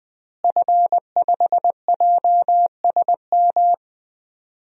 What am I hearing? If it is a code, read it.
Morse code, heard as a keyed tone, about 20 wpm, 700 Hz. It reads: F5JSM
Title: DXCC-100 Call Signs - 5 Characters - Repeated 3x - 20wpm